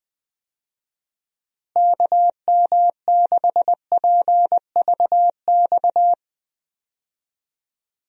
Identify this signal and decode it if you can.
Morse code, heard as a keyed tone, about 21 wpm, 700 Hz. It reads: KM6PVX
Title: USA Call Signs - 6 Characters - Repeated 3x - 20wpm